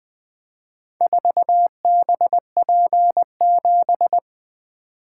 Morse code, heard as a keyed tone, about 20 wpm, 700 Hz. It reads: 4BP7
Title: Four Letter-Number Combinations - 20wpm